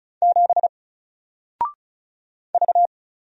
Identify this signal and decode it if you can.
Morse code, heard as a keyed tone, about 36 wpm, 700 Hz. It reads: 7 V
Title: Single Character (Letter:Number:Punctuation:Prosign) 35wpm